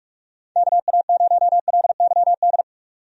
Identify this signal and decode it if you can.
Morse code, heard as a keyed tone, about 46 wpm, 700 Hz. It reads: KA0LYD
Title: USA Call Signs - 6 Characters - 45wpm